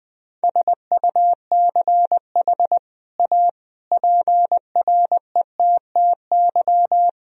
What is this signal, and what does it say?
Morse code, heard as a keyed tone, about 20 wpm, 700 Hz. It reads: SUCH A PRETTY
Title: Sentences from Top 1000 Words - Review Part 2 - 20wpm